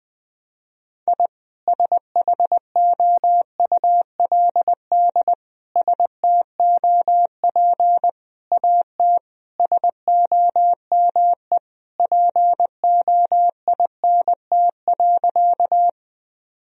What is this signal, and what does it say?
Morse code, heard as a keyed tone, about 20 wpm, 700 Hz. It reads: I SHOULD STOP AT SOME POINT.
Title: Sentences from Top 1000 Words - Review Part 1 - 20wpm